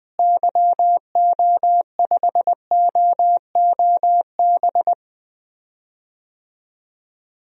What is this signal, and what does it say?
Morse code, heard as a keyed tone, about 20 wpm, 700 Hz. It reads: YO5OOB
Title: DXCC-100 Call Signs - 6 Characters - Repeated 3x - 20wpm